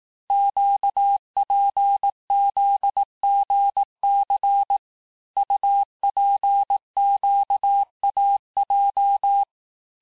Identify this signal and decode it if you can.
Morse code, heard as a keyed tone, about 18 wpm, 800 Hz. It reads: QPZGC UPQAJ
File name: cours-du-lundi files / cours-1588-texte-chiffre-vitesse-17MPM